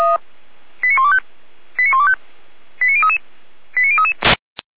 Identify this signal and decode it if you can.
Five-tone selective call (ZVEI), heard as quick runs of five tones, tones of about 70 ms. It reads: 89125 89125 89033 89033
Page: /455/SDR/ZVEI/